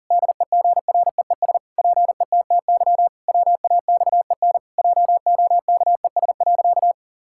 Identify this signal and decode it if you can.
Morse code, heard as a keyed tone, about 40 wpm, 700 Hz. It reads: DEGREES PETTY WAXEN JOKES.
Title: Pangram Challenge v2 - 40wpm